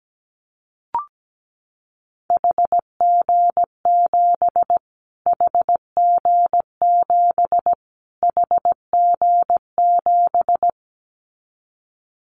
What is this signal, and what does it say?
Morse code, heard as a keyed tone, about 17 wpm, 700 Hz. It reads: HG7 HG7 HG7
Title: Callsign Prefixes for Top 25 DXCC - Repeated 3x - 17wpm